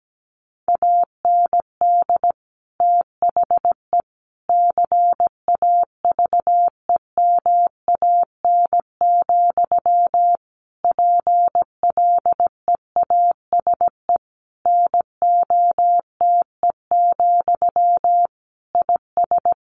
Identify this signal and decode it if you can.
Morse code, heard as a keyed tone, about 17 wpm, 700 Hz. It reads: AND THE CAVEMAN, PLEASE NOTE, IS
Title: A Sound Of Thunder 17wpm